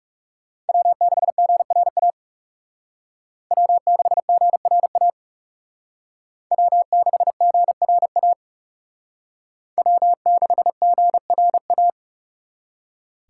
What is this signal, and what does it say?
Morse code, heard as a keyed tone, about 36 wpm, 700 Hz. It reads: W6GRA W6GRA W6GRA W6GRA
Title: Speed Racing - USA 1x3 Call Signs 30wpm